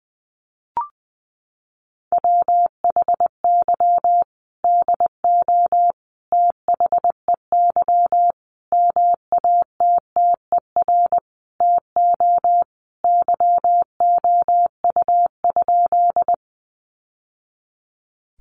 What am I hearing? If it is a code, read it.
Morse code, heard as a keyed tone, about 20 wpm, 700 Hz. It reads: WHY DO THEY MATTER TO YOU?
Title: Sentences from Top 700 Words 20wpm